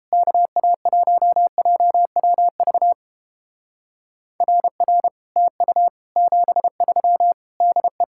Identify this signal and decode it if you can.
Morse code, heard as a keyed tone, about 31 wpm, 700 Hz. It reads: KA1JWV RR TU 73 DE